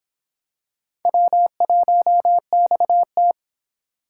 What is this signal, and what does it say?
Morse code, heard as a keyed tone, about 24 wpm, 700 Hz. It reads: W1XT